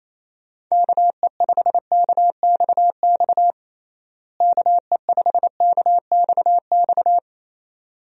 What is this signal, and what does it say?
Morse code, heard as a keyed tone, about 27 wpm, 700 Hz. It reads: KE5KXX KE5KXX
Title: USA Call Signs - 6 Characters - Repeated 2x - 28wpm